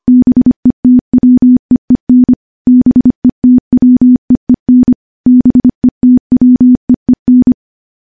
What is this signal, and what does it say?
Morse code, heard as a keyed tone, about 25 wpm, 250 Hz. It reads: BETWEEN BETWEEN BETWEEN
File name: morse / morse-training-25WPM